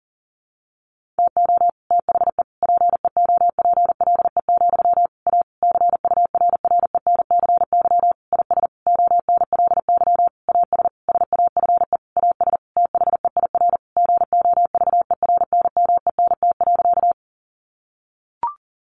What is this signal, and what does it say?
Morse code, heard as a keyed tone, about 40 wpm, 700 Hz. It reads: TO THE PEOPLE, A CURRENCY IS ONLY AS SAFE AS THEIR GOVERNMENT.
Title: Sentences from Top 1500 Words 40wpm